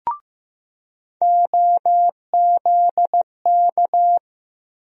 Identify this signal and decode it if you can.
Morse code, heard as a keyed tone, about 15 wpm, 700 Hz. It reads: OZK
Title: USA Call Sign Postfixes - 3 Letters 15wpm